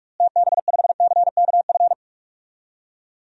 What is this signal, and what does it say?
Morse code, heard as a keyed tone, about 47 wpm, 700 Hz. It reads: TB5CKF
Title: DXCC-100 Call Signs - 6 Characters - Repeated 2x - 45wpm